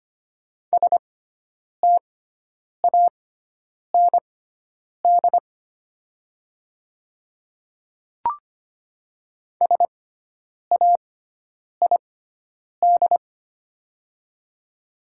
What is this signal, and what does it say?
Morse code, heard as a keyed tone, about 25 wpm, spaced as for 9 wpm, 700 Hz. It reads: STAND SAID
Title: 6x Character Spacing - Top 500 Words - 25wpm